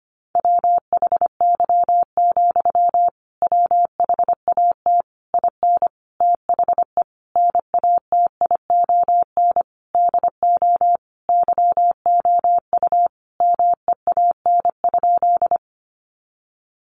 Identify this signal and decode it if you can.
Morse code, heard as a keyed tone, about 25 wpm, 700 Hz. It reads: WHY, WHAT IN THE NATION DO YOU MEAN?
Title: Sentences from Top 900 Words 25wpm